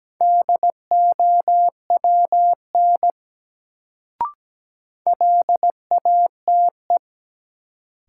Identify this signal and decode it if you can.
Morse code, heard as a keyed tone, about 17 wpm, 700 Hz. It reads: DOWN LATE
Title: Sentences from Top 300 Words - New Word Drill - 17wpm